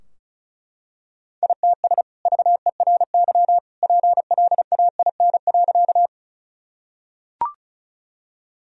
Morse code, heard as a keyed tone, about 34 wpm, 700 Hz. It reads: ITS VERY PLAIN.
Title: Sentences from Top 1000 Words - Review Part 2 - 35wpm